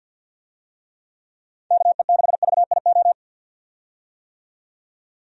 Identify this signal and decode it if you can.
Morse code, heard as a keyed tone, about 46 wpm, 700 Hz. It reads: KE6VIO
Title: USA Call Signs - 6 Characters - 50wpm